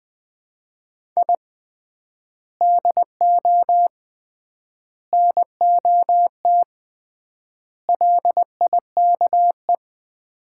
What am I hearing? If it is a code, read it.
Morse code, heard as a keyed tone, about 20 wpm, 700 Hz. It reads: I DO NOT LIKE
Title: 3x Word-Spacing - Sets of 4 Words - 20wpm